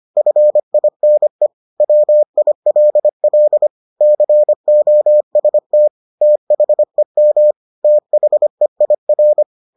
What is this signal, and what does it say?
Morse code, heard as a keyed tone, about 25 wpm, 600 Hz. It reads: FINE WILL COST THEM THEIR